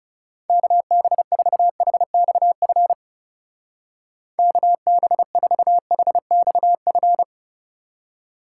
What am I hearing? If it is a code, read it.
Morse code, heard as a keyed tone, about 38 wpm, 700 Hz. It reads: KB4HXF KB4HXF
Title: Speed Racing - USA 2x3 Call Signs 30wpm